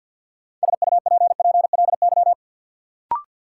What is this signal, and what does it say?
Morse code, heard as a keyed tone, about 53 wpm, 700 Hz. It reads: SUPPLY
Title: Sentences from Top 1200 Words - New Word Drill Alt - 50wpm